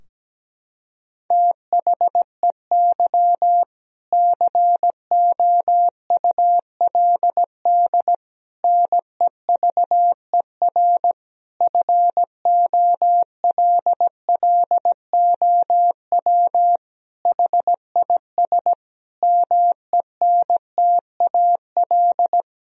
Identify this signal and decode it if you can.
Morse code, heard as a keyed tone, about 17 wpm, 700 Hz. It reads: THEY COULD NEVER FOLLOW HIS MENTAL
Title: Sentences from Top 1000 Words - Review Part 4 - 17wpm